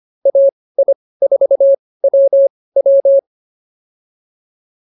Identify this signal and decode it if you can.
Morse code, heard as a keyed tone, about 24 wpm, spaced as for 18 wpm, 550 Hz. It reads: AI4WW